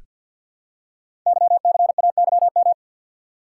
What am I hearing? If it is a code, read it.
Morse code, heard as a keyed tone, about 48 wpm, 700 Hz. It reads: YCAYM
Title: Code Groups - 5 Letters - 50wpm